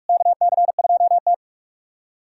Morse code, heard as a keyed tone, about 48 wpm, 700 Hz. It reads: KC2T
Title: USA Callsign DE Callsign - 4 Characters - 45wpm